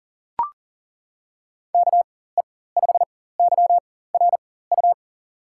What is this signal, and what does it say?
Morse code, heard as a keyed tone, about 40 wpm, spaced as for 19 wpm, 700 Hz. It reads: KE5YRU
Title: USA Call Signs - 6 Characters - 4x Character Spacing - 40wpm